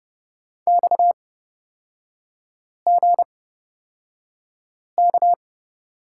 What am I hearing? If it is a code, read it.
Morse code, heard as a keyed tone, about 30 wpm, spaced as for 5 wpm, 700 Hz. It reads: XGK
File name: Lesson 37e - Letter x - Three Characters 30wpm